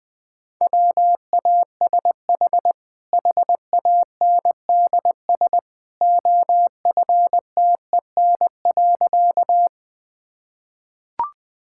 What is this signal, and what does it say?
Morse code, heard as a keyed tone, about 21 wpm, 700 Hz. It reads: WASH HANDS OFTEN.
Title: Easy Sentences with Hints - 20wpm